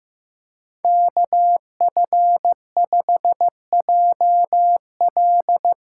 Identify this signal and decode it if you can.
Morse code, heard as a keyed tone, about 15 wpm, 700 Hz. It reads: KF5JL
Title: USA Call Signs - 5 Characters - 4x Character Spacing - 15wpm